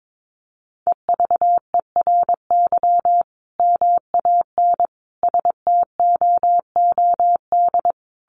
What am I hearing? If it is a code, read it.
Morse code, heard as a keyed tone, about 22 wpm, 700 Hz. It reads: EVERY MAN STOOD
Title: Sentences from Top 1400 Words 22wpm